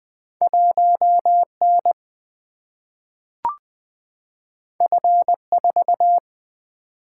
Morse code, heard as a keyed tone, about 20 wpm, 700 Hz. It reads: 1 N F 4
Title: Single-Letter Single-Number 20wpm